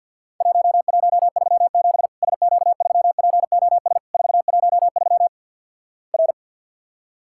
Morse code, heard as a keyed tone, about 50 wpm, 700 Hz. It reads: 1137 SQ3POS 413 R
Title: Worked All Europe DX Contest - QTC Exchange - 50wpm